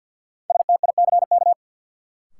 Morse code, heard as a keyed tone, about 50 wpm, 700 Hz. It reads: STICK
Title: Sentences from Top 1200 Words - New Word Drill Alt - 50wpm